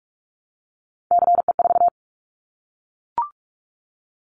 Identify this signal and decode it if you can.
Morse code, heard as a keyed tone, about 44 wpm, 700 Hz. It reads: CE4
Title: Callsign Prefixes for Top 100 DXCC - 45wpm